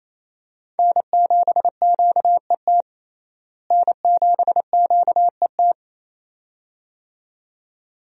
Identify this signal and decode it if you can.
Morse code, heard as a keyed tone, about 28 wpm, 700 Hz. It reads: N7QET N7QET
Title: USA Call Signs - 5 Characters - Repeated 2x - 28wpm